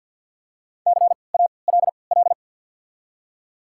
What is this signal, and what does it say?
Morse code, heard as a keyed tone, about 51 wpm, spaced as for 28 wpm, 700 Hz. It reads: CALL
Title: 3x Character Spacing - Top 200 Words - 50wpm